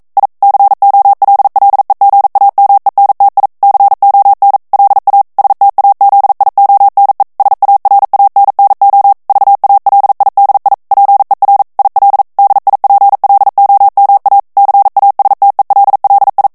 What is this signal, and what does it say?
Morse code, heard as a keyed tone, about 42 wpm, 800 Hz. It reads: I COLLEGAMENTI CON LA STAZIONE SARANNO VALIDI PER IL DIPLOMA CASTELLI
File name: ITALIEN texte clair 1 vitesse 40 mots minute